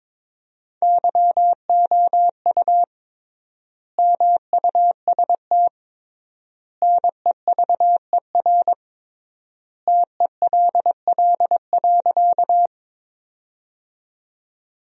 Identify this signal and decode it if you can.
Morse code, heard as a keyed tone, about 22 wpm, 700 Hz. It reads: YOU MUST NEVER TELL.
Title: 3x Word-Spacing - Sentences from Top 300 Words - 22wpm